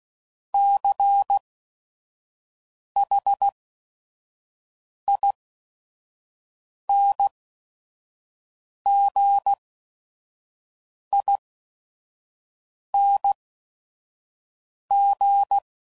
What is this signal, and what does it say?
Morse code, heard as a keyed tone, about 16 wpm, spaced as for 5 wpm, 800 Hz. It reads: CHINGING